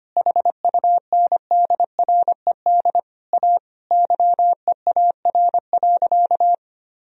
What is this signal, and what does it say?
Morse code, heard as a keyed tone, about 25 wpm, 700 Hz. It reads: HUNDRED A YEAR.
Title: Sentences from Top 1000 Words - Review Part 2 - 25wpm